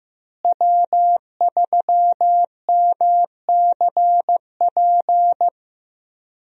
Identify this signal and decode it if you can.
Morse code, heard as a keyed tone, about 15 wpm, 700 Hz. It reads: W3MCP
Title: USA Call Signs - 5 Characters - Repeated 3x - 15wpm